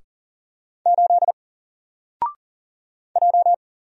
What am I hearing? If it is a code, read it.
Morse code, heard as a keyed tone, about 38 wpm, 700 Hz. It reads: 8 J
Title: Single Character (Letter:Number:Punctuation:Prosign) 40wpm